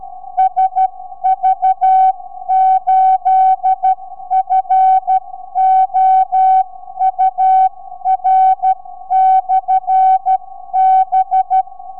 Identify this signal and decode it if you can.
Morse code, heard as a keyed tone, about 12 wpm, 760 Hz. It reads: SV8FOUR/B
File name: sv8four 4m